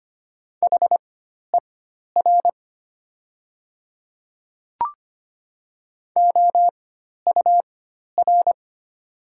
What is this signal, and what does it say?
Morse code, heard as a keyed tone, about 25 wpm, spaced as for 12 wpm, 700 Hz. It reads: HER OUR
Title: 4x Character Spacing - Top 200 Words - 25wpm